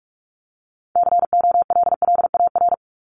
Morse code, heard as a keyed tone, about 44 wpm, 700 Hz. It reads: COLLAR
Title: Top 1751-2000 Words 45wpm